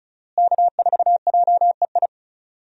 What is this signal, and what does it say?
Morse code, heard as a keyed tone, about 36 wpm, 700 Hz. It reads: K4JEI
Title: USA Call Signs - 5 Characters - Repeated 2x - 35wpm